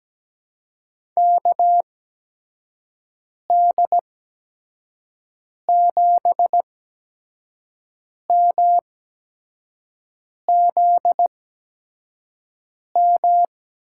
Morse code, heard as a keyed tone, about 17 wpm, spaced as for 5 wpm, 700 Hz. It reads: KD7MZM
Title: USA Call Signs - 8x Character Spacing - 17wpm